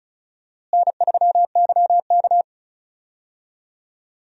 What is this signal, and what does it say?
Morse code, heard as a keyed tone, about 34 wpm, 700 Hz. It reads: N3YK
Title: USA Call Signs - 4 Characters - Repeated 2x - 35wpm